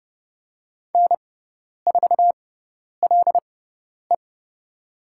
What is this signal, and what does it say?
Morse code, heard as a keyed tone, about 29 wpm, spaced as for 10 wpm, 700 Hz. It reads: N4LE
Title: USA Call Signs - 4 Characters - 6x Character Spacing - 30wpm